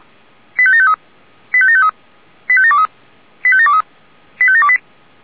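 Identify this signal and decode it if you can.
Five-tone selective call (ZVEI), heard as quick runs of five tones, tones of about 70 ms. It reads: 85753 85753 85723 85723 85728
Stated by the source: ZVEI five-tone selective call 85726, received with an SDR.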